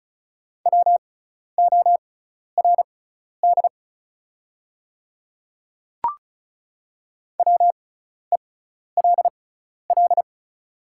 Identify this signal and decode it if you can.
Morse code, heard as a keyed tone, about 36 wpm, spaced as for 12 wpm, 700 Hz. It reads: WORD WELL
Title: 6x Character Spacing - Top 200 Words - 35wpm